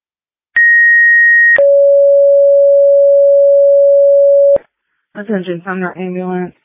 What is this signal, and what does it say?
Two-tone page, heard as a one-second tone, then a three-second tone, about 1900 then 560 Hz, with speech. It is heard from 0.6 to 4.6 s.